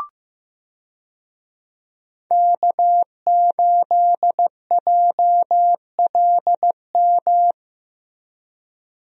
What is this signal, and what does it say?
Morse code, heard as a keyed tone, about 15 wpm, 700 Hz. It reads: K8JLM
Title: USA Call Signs - 5 Characters - Repeated 2x - 15wpm